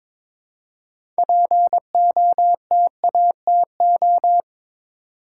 Morse code, heard as a keyed tone, about 22 wpm, 700 Hz. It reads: POTATO
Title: Six Letter Words - 22wpm